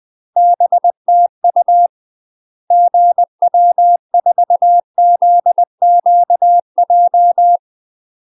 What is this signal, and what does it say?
Morse code, heard as a keyed tone, about 20 wpm, 700 Hz. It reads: BTU GW4ZQJ
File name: UK MASTER QSOs TRAINING/20WPM